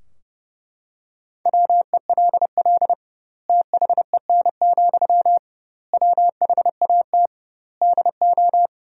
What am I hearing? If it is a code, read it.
Morse code, heard as a keyed tone, about 32 wpm, 700 Hz. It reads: WELL THEN, WHAT DO
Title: 2x Word-Spacing - Sentences from Top 200 Words - 30wpm